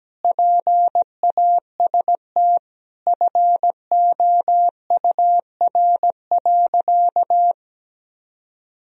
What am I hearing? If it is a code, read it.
Morse code, heard as a keyed tone, about 17 wpm, 700 Hz. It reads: PAST FOUR.
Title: Sentences from Top 1000 Words - Review Part 3 - 17wpm